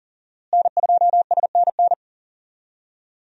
Morse code, heard as a keyed tone, about 40 wpm, 690 Hz. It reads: N2SNN
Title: USA Call Signs - 5 Characters - Repeated 3x - 40wpm